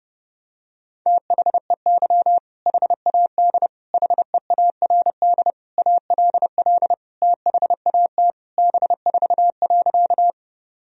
Morse code, heard as a keyed tone, about 30 wpm, 700 Hz. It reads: THEY HAD HEARD ALL THAT B4.
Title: Sentences from Top 400 Words and Abbreviations 30wpm